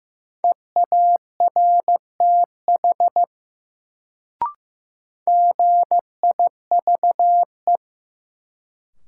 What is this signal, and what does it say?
Morse code, heard as a keyed tone, about 15 wpm, 700 Hz. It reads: EARTH GIVE
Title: Sentences from Top 200 Words - New Word Drill - 15wpm